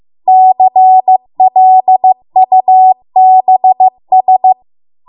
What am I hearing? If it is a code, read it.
Morse code, heard as a keyed tone, about 15 wpm, 750 Hz. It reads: CLUBS